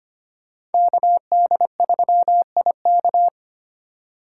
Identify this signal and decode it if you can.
Morse code, heard as a keyed tone, about 33 wpm, 700 Hz. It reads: KD3IK